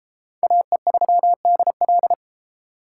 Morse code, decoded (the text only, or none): AE3DL